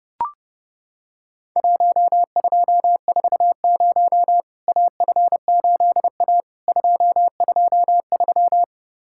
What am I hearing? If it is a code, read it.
Morse code, heard as a keyed tone, about 30 wpm, 700 Hz. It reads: 1240 AF8A 223
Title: Worked All Europe DX Contest - QTC Exchange - 30wpm